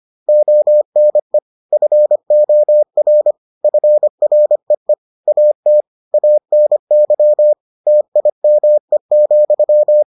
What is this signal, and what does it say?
Morse code, heard as a keyed tone, about 25 wpm, 600 Hz. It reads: ONE FOR FREE AT ANY TIME,